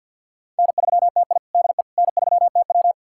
Morse code, heard as a keyed tone, about 53 wpm, 700 Hz. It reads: N3TI DE N3TW
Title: USA Callsign DE Callsign - 4 Characters - 50wpm